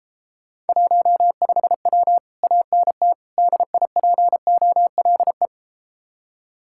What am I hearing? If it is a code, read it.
Morse code, heard as a keyed tone, about 33 wpm, 700 Hz. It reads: 15W ANT DIPOLE